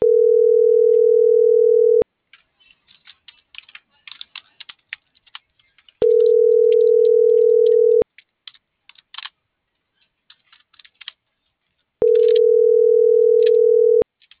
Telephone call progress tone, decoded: ringback tone

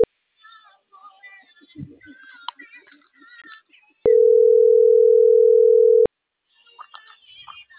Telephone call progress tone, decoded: ringback tone

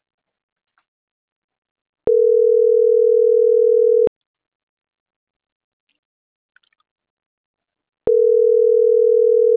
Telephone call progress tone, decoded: ringback tone

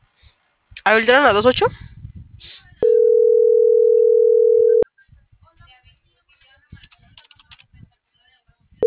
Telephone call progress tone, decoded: ringback tone

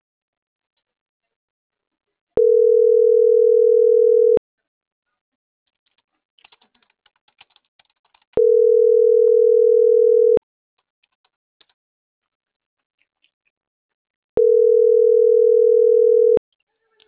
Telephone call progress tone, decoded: ringback tone

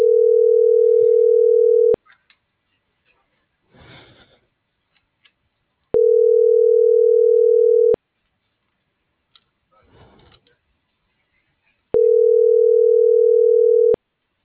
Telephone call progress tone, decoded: ringback tone

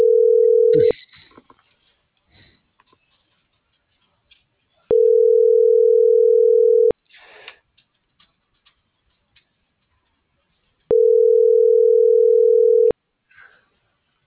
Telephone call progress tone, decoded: ringback tone

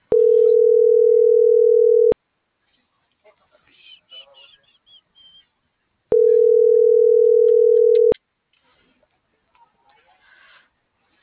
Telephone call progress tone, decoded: ringback tone

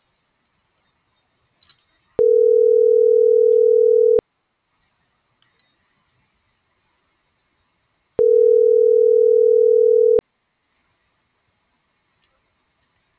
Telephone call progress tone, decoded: ringback tone

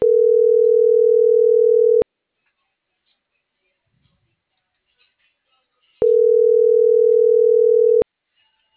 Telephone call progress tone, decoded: ringback tone

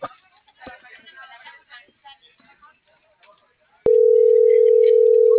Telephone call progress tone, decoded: ringback tone

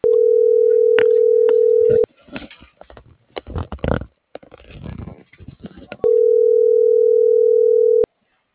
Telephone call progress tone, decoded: ringback tone

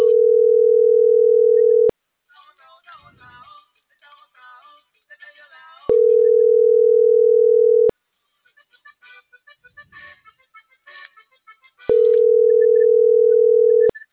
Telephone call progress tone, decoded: ringback tone

